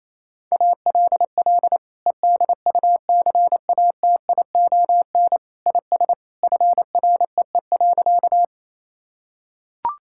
Morse code, heard as a keyed tone, about 28 wpm, 700 Hz. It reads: ALL EDUCATION IS FREE.
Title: Sentences from Top 1000 Words - Review Part 6 - 28wpm